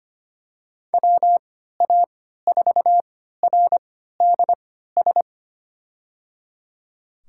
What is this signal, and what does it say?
Morse code, heard as a keyed tone, about 25 wpm, spaced as for 14 wpm, 700 Hz. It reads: WA4RDS
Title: USA Call Signs - 3x Character Spacing - 25wpm